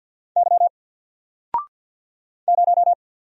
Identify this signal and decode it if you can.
Morse code, heard as a keyed tone, about 48 wpm, 700 Hz. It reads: Y 0
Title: Single Letter-Number Alt 50wpm